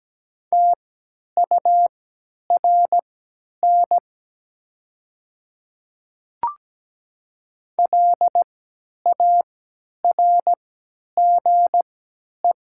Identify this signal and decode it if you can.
Morse code, heard as a keyed tone, about 17 wpm, spaced as for 10 wpm, 700 Hz. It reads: TURN LARGE
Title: 3x Character Spacing - Top 200 Words - 17wpm